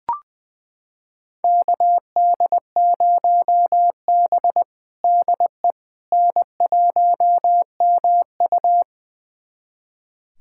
Morse code, heard as a keyed tone, about 20 wpm, 700 Hz. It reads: KD0B DE N1MU
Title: USA Callsign DE Callsign - 4 Characters - 20wpm